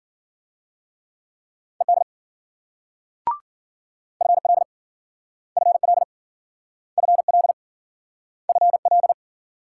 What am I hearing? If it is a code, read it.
Morse code, heard as a keyed tone, about 53 wpm, 700 Hz. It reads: ED FL FL FL FL